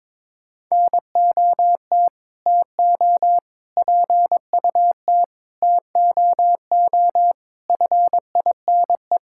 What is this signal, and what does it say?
Morse code, heard as a keyed tone, about 21 wpm, 700 Hz. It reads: NOT TO PUT TOO FINE
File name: practice / 22wpm / Sets of 8 Words - 22wpm